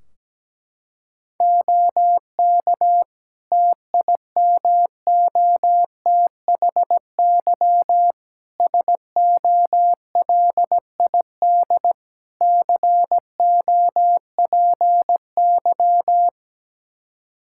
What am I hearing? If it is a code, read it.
Morse code, heard as a keyed tone, about 17 wpm, 700 Hz. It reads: OK TIMOTHY SOLID COPY
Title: K1USN SST QSO Practice 17wpm